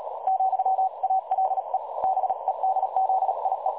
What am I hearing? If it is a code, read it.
Morse code, heard as a keyed tone, about 38 wpm, 760 Hz, with noise. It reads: , ALE BEDZIE
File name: dlazaawansowanych / nagraniazpasma / websdr recording 3565.0kHz